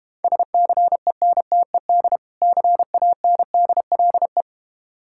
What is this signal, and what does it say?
Morse code, heard as a keyed tone, about 32 wpm, 690 Hz. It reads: SCENTED CANDLE